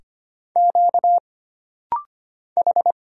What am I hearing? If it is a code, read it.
Morse code, heard as a keyed tone, about 25 wpm, 700 Hz. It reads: Q H
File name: Single Character (Letter:Number:Punctuation:Prosign) - Rapid-Fire - 25wpm